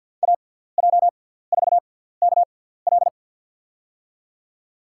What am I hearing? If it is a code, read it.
Morse code, heard as a keyed tone, about 53 wpm, spaced as for 17 wpm, 700 Hz. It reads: AJ4KL
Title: USA Call Signs - 5 Characters - 6x Character Spacing - 50wpm